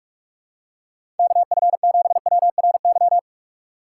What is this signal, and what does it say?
Morse code, heard as a keyed tone, about 46 wpm, 700 Hz. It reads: KF7WRY
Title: USA Call Signs - 3x Character Spacing - 45wpm